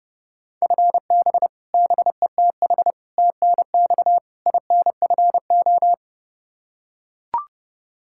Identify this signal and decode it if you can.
Morse code, heard as a keyed tone, about 32 wpm, 700 Hz. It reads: FB BETH TNX INFO
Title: Practice QSO - Exchange 3 - More Info - 30wpm